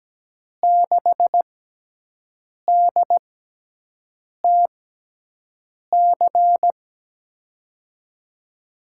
Morse code, heard as a keyed tone, about 17 wpm, spaced as for 6 wpm, 700 Hz. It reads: 6DTC
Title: USA Call Signs - 6x Character Spacing - 17wpm